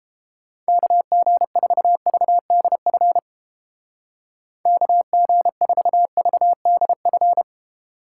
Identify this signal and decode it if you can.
Morse code, heard as a keyed tone, about 29 wpm, 700 Hz. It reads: KG4VDF KG4VDF